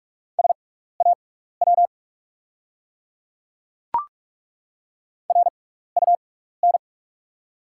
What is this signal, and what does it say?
Morse code, heard as a keyed tone, about 43 wpm, spaced as for 15 wpm, 690 Hz. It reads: SAW RUN
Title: 6x Character Spacing - Top 500 Words - 45wpm